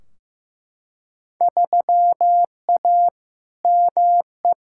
Morse code, heard as a keyed tone, about 15 wpm, 700 Hz. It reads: 3A ME
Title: Field Day Exchanges - 15wpm